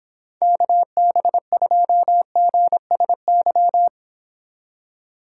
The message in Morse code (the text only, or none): KB2GSY